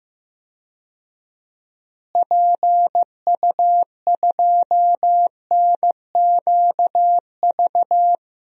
Morse code, heard as a keyed tone, about 15 wpm, 700 Hz. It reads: PU2NQV
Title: DXCC-100 Call Signs - 6 Characters - Repeated 3x - 15wpm